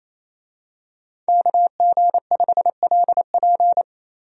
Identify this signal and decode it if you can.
Morse code, heard as a keyed tone, about 28 wpm, 700 Hz. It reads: KG5LP
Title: USA Call Signs - 5 Characters - Repeated 3x - 28wpm